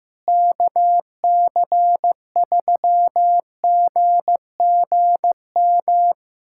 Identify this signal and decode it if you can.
Morse code, heard as a keyed tone, about 15 wpm, 700 Hz. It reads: KC3GGM
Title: USA Callsign DE Callsign - 6 Characters - 15wpm